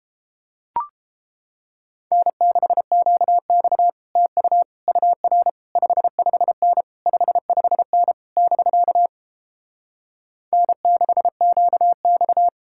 Morse code, heard as a keyed tone, about 33 wpm, 700 Hz. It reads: N6QX TU UR 55N 55N <BK> N6QX